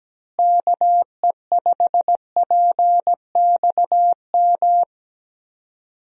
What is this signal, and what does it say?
Morse code, heard as a keyed tone, about 17 wpm, 700 Hz. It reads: KE5PXM
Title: USA Call Signs - 6 Characters - Repeated 3x - 17wpm